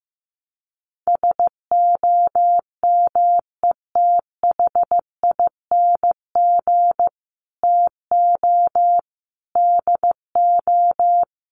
Morse code, heard as a keyed tone, about 15 wpm, 700 Hz. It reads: SOMETHING TO DO
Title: Sets of 5 Words Encore 15wpm